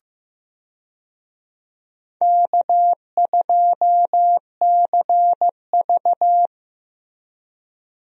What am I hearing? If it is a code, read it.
Morse code, heard as a keyed tone, about 15 wpm, 700 Hz. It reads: K2CV
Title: USA Call Signs - 4 Characters - Repeated 3x - 15wpm